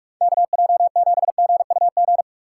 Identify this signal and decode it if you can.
Morse code, heard as a keyed tone, about 46 wpm, 700 Hz. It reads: KJ7GUG
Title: USA Call Signs - 6 Characters - 3x Character Spacing - 45wpm